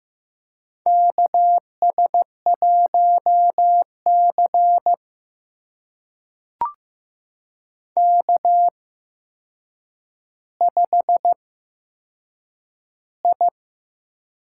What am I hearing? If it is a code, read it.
Morse code, heard as a keyed tone, about 15 wpm, spaced as for 7 wpm, 700 Hz. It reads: KS1C K5I
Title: USA Call Signs - 4 Characters - 8x Character Spacing - 15wpm